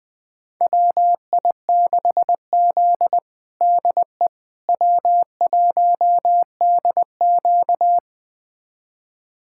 Morse code, heard as a keyed tone, about 20 wpm, 700 Hz. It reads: WI6Z DE W1DQ
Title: USA Callsign DE Callsign - 4 Characters - 20wpm